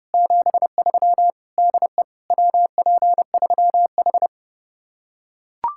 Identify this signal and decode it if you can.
Morse code, heard as a keyed tone, about 30 wpm, 700 Hz. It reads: 73 DE WP3H